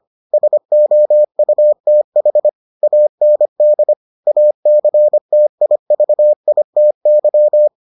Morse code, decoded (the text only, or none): SOUTH AND ACTIVITY